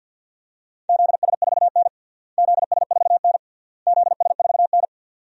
Morse code, heard as a keyed tone, about 50 wpm, 700 Hz. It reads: ZS4N ZS4N ZS4N
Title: DXCC-100 Call Signs - 4 Characters - Repeated 3x - 50wpm